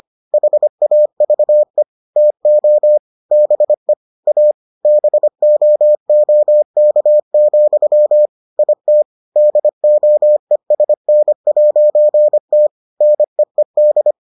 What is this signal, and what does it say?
Morse code, heard as a keyed tone, about 25 wpm, 600 Hz. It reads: HAVE TO BE A BOOK, IT DOESN'T NEED